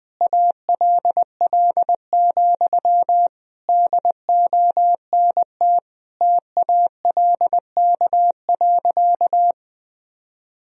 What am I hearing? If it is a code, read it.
Morse code, heard as a keyed tone, about 20 wpm, 700 Hz. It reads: ALL, DONT TALK.
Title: Sentences from Top 1000 Words - Review Part 2 - 20wpm